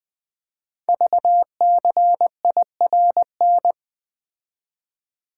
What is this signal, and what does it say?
Morse code, heard as a keyed tone, about 20 wpm, 700 Hz. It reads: VCIRN
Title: Code Groups - 5 Letters - 20wpm